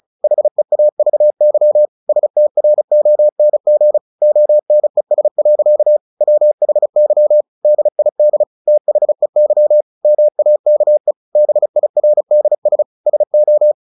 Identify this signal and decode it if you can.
Morse code, heard as a keyed tone, about 34 wpm, 600 Hz. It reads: HEAVY STRONG ONES. WHY DID THEY MAKE BIRDS SO
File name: the old man and the sea / the old man and the sea 1 35wpm